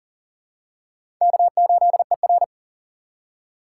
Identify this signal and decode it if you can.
Morse code, heard as a keyed tone, about 40 wpm, 690 Hz. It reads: K8ER
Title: USA Call Signs - 4 Characters - 2x Character Spacing - 40wpm